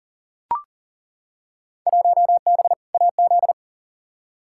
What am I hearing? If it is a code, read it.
Morse code, heard as a keyed tone, about 37 wpm, 700 Hz. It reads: 1B AZ
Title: Field Day Exchanges - 40wpm